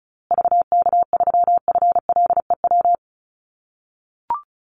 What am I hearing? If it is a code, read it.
Morse code, heard as a keyed tone, about 34 wpm, 700 Hz. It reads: VK3FLEW
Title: USA Call Signs Repeated 3x 35wpm